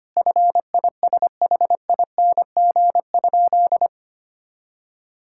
Morse code, heard as a keyed tone, about 25 wpm, 700 Hz. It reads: FISHING?
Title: Sentences from Top 1000 Words - Review Part 5 - 25wpm